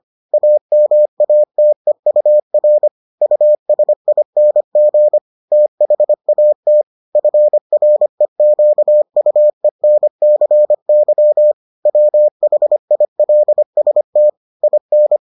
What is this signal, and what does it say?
Morse code, heard as a keyed tone, about 25 wpm, 600 Hz. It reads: AMATEUR USING THAT FREQUENCY WHILST IN